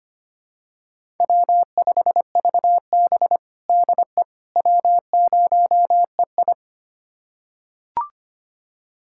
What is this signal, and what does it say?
Morse code, heard as a keyed tone, about 25 wpm, 700 Hz. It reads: W5VB DE W0EI
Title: USA Callsign DE Callsign - 4 Characters - 25wpm